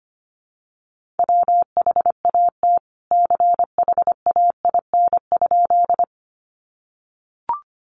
Morse code, heard as a keyed tone, about 25 wpm, 700 Hz. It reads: WHAT CHAIN?
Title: Sentences from Top 1500 Words 25wpm